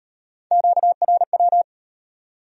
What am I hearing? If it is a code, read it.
Morse code, heard as a keyed tone, about 35 wpm, 700 Hz. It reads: QRW